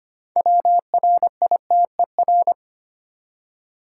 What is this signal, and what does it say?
Morse code, heard as a keyed tone, about 25 wpm, 700 Hz. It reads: WRITER
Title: Six Letter Words - 25wpm